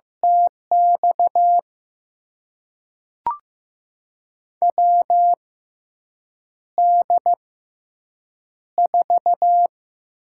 Morse code, heard as a keyed tone, about 15 wpm, spaced as for 7 wpm, 700 Hz. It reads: TX WD4
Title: USA Call Signs - 6 Characters - 6x Character Spacing - 15wpm